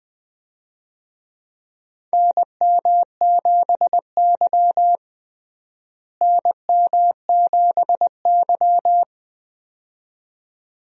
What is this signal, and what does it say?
Morse code, heard as a keyed tone, about 20 wpm, 700 Hz. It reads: NM7Y NM7Y
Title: USA Call Signs - 4 Characters - Repeated 2x - 20wpm